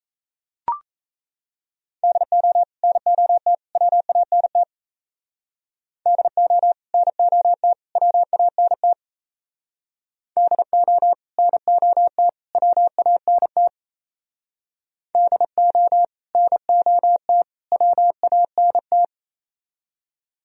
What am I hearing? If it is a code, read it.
Morse code, heard as a keyed tone, about 33 wpm, 700 Hz. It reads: DO NOT WANT DO NOT WANT DO NOT WANT DO NOT WANT
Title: Speed-Racing - Sets of 3 Words 28wpm